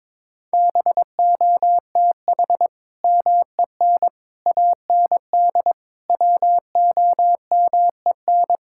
Morse code, heard as a keyed tone, about 22 wpm, 700 Hz. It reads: BOTH MEN AND WOMEN
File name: practice / 22wpm / Sentences with 3-Word Phrases - 22wpm